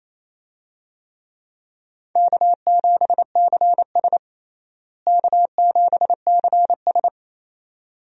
Morse code, heard as a keyed tone, about 28 wpm, 700 Hz. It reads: K7CS K7CS
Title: USA Call Signs - 4 Characters - Repeated 2x - 28wpm